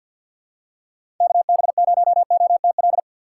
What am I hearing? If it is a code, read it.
Morse code, heard as a keyed tone, about 53 wpm, 700 Hz. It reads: KB0OTL
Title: USA Call Signs - 6 Characters - 3x Character Spacing - 50wpm